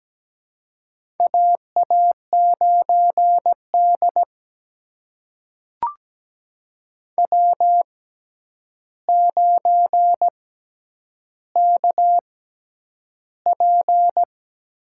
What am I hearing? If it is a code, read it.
Morse code, heard as a keyed tone, about 17 wpm, spaced as for 9 wpm, 700 Hz. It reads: AA9D W9KP
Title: USA Call Signs - 4 Characters - 6x Character Spacing - 17wpm